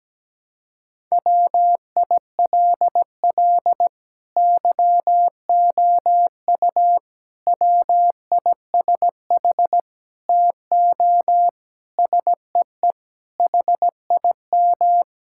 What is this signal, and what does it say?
Morse code, heard as a keyed tone, about 17 wpm, 700 Hz. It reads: WILL YOU WISH TO SEE HIM
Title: Sentences from Top 1600 Words 17wpm